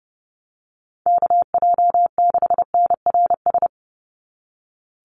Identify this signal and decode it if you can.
Morse code, heard as a keyed tone, about 30 wpm, 700 Hz. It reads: KJ6NRS
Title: USA Call Signs 30wpm